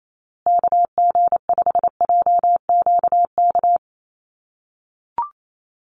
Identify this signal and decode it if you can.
Morse code, heard as a keyed tone, about 28 wpm, 700 Hz. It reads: KG5JQK
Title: USA Call Signs Repeated 2x 28wpm